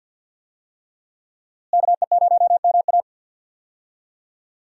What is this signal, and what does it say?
Morse code, heard as a keyed tone, about 51 wpm, 700 Hz. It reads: KE0MA
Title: USA Call Signs - 5 Characters - Repeated 3x - 50wpm